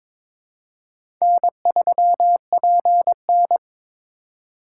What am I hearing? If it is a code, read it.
Morse code, heard as a keyed tone, about 22 wpm, 690 Hz. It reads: N3PN
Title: USA Call Signs - 4 Characters - 3x Character Spacing - 22wpm